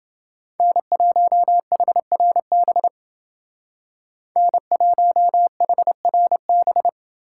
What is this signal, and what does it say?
Morse code, heard as a keyed tone, about 29 wpm, 700 Hz. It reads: N1HRB N1HRB